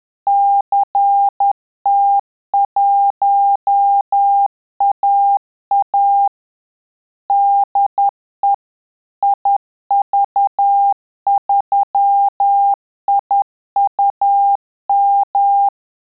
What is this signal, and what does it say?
Morse code, heard as a keyed tone, about 10 wpm, 800 Hz. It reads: CT1AA DE IV3IUM
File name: Telegraphy-group / QSO CW 10 PPM